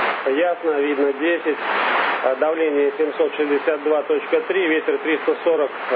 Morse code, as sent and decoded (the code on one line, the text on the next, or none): none